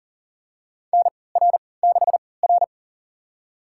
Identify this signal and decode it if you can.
Morse code, heard as a keyed tone, about 37 wpm, spaced as for 22 wpm, 700 Hz. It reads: NR6R
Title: USA Call Signs - 4 Characters - 3x Character Spacing - 40wpm